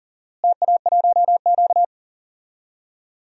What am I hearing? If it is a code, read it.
Morse code, heard as a keyed tone, about 40 wpm, 700 Hz. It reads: TA1Q
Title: DXCC-100 Call Signs - 4 Characters - Repeated 2x - 40wpm